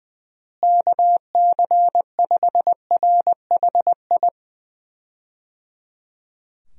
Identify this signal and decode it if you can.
Morse code, heard as a keyed tone, about 20 wpm, 700 Hz. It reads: KC5RHI
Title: USA Call Signs - 6 Characters - Repeated 3x - 20wpm